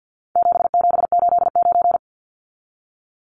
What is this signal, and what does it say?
Morse code, heard as a keyed tone, about 48 wpm, 700 Hz. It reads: 7789
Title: Numbers - 4-Digit 50wpm